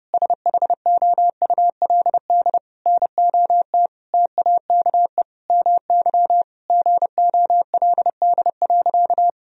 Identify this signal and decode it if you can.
Morse code, heard as a keyed tone, about 32 wpm, 700 Hz. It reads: SHOULD NOT TAKE MY GOLD.
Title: Sentences from Top 1000 Words - Review Part 3 - 30wpm